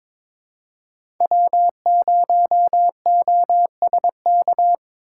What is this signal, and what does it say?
Morse code, heard as a keyed tone, about 21 wpm, 700 Hz. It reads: W0OSK
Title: USA Call Signs - 5 Characters - 8x Character Spacing - 22wpm